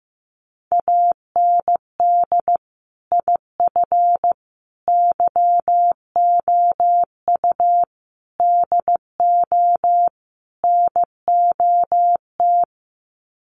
Morse code, heard as a keyed tone, about 15 wpm, 700 Hz. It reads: AND IF YOU DO NOT
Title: Sets of 5 Words 15wpm